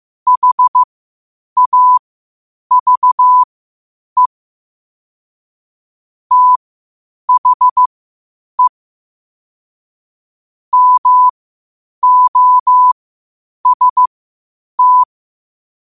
Morse code, heard as a keyed tone, about 15 wpm, spaced as for 8 wpm, 1000 Hz. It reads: HAVE THE MOST
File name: qotdfiles / 8wpm